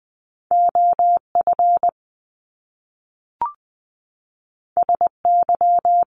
Morse code, heard as a keyed tone, about 20 wpm, 700 Hz. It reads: OF SY
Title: Top Two Letter Combinations in Words 20wpm